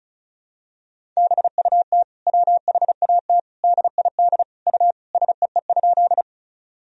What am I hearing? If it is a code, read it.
Morse code, heard as a keyed tone, about 36 wpm, 700 Hz. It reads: BUT WHAT DID U SEE?
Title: Sentences from Top 100 Words and Abbreviations 35wpm